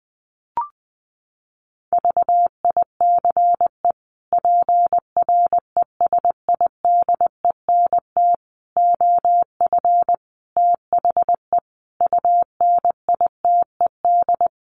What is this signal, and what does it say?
Morse code, heard as a keyed tone, about 19 wpm, 700 Hz. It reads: VICE PRESIDENT OF THE UNITED
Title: Sets of 5 Words Encore2 20wpm